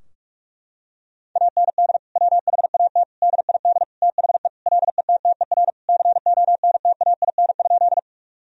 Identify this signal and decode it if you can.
Morse code, heard as a keyed tone, about 41 wpm, 700 Hz. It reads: AND WHAT DID THE LETTER CONTAIN?
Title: Sentences from Top 1000 Words - Review Part 2 - 45wpm